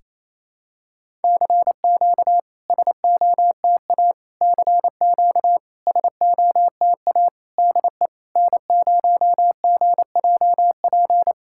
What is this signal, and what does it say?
Morse code, heard as a keyed tone, about 28 wpm, 690 Hz. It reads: CQ SOTA CQ SOTA DE N0GJP
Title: SOTA QSO Practice 28wpm